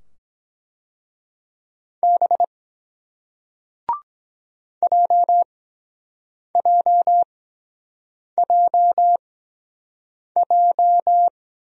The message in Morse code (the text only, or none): B JJJJ